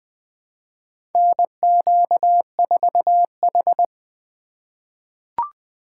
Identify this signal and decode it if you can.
Morse code, heard as a keyed tone, about 20 wpm, 700 Hz. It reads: NQ4H
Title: USA Call Signs - 3x Character Spacing - 20wpm